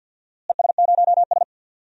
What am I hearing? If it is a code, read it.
Morse code, heard as a keyed tone, about 51 wpm, 700 Hz. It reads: ES0S
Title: DXCC-100 Call Signs - 4 Characters - Repeated 3x - 50wpm